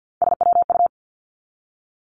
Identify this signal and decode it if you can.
Morse code, heard as a keyed tone, about 50 wpm, 700 Hz. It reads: SWU
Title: Three Challenging Letter-Number Combinations 50wpm